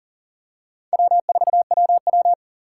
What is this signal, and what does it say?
Morse code, heard as a keyed tone, about 40 wpm, 700 Hz. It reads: W4WW
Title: USA Call Signs - 4 Characters - 8x Character Spacing - 40wpm